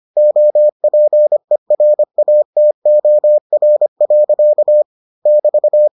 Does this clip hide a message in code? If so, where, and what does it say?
Yes: Morse code from 0.2 to 5.9 s, heard as a keyed tone, about 25 wpm, 600 Hz. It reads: OPERATOR. <BT>